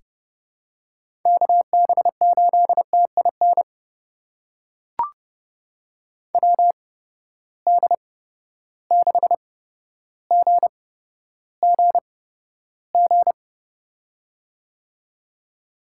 Morse code, heard as a keyed tone, about 30 wpm, spaced as for 13 wpm, 700 Hz. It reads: KB8TIN WD6GGG
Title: USA Call Signs - 6 Characters - 8x Character Spacing - 30wpm